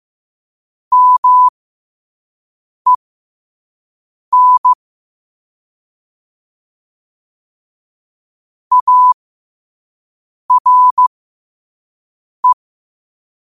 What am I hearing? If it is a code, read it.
Morse code, heard as a keyed tone, about 15 wpm, spaced as for 5 wpm, 1000 Hz. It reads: MEN ARE
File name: qotdfiles / 5wpm